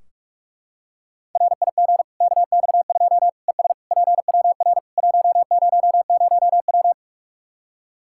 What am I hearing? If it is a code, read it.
Morse code, heard as a keyed tone, about 41 wpm, 700 Hz. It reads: RIG KX2 ES PWR 100W
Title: K1USN SST QSO Practice 45wpm